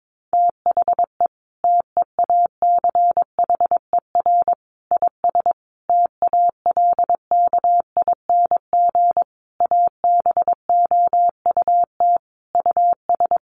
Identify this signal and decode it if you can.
Morse code, heard as a keyed tone, about 22 wpm, 700 Hz. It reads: THE TEACHER IS TALKING ABOUT US
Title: Easy Phrases 22wpm